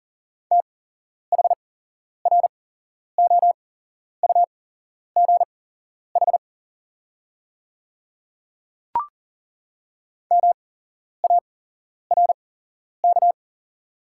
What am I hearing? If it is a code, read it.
Morse code, heard as a keyed tone, about 40 wpm, spaced as for 11 wpm, 700 Hz. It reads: THROUGH MARK
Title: 8x Character Spacing - Top 300 Words - 40wpm